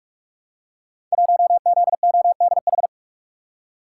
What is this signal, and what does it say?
Morse code, heard as a keyed tone, about 44 wpm, 700 Hz. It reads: 1ZODH
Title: Code Groups - 5 Letter:Numbers - 45wpm